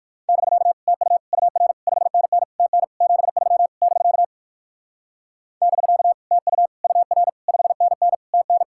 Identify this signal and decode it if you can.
Morse code, heard as a keyed tone, about 58 wpm, 700 Hz. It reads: <BK> TU UR 5NN TN 73 <BK> <BK> TU UR 5NN TN